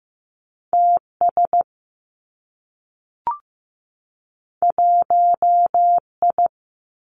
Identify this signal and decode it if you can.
Morse code, heard as a keyed tone, about 15 wpm, 700 Hz. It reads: T S 1 I
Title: Two Challenging Letter-Number Combinations 15wpm